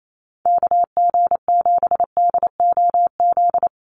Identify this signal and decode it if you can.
Morse code, heard as a keyed tone, about 28 wpm, 700 Hz. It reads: KG7DOZ
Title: USA Call Signs 28wpm